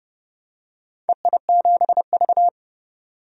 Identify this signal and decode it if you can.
Morse code, heard as a keyed tone, about 30 wpm, 700 Hz. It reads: EI7V
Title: DXCC-100 Call Signs - 4 Characters - 30wpm